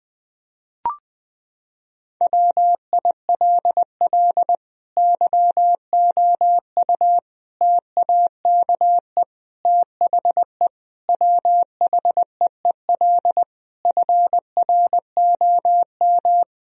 Morse code, heard as a keyed tone, about 20 wpm, 700 Hz. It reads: WILL YOU TAKE THE WHEEL FROM
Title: Sentences from Top 1000 Words - Review Part 3 - 20wpm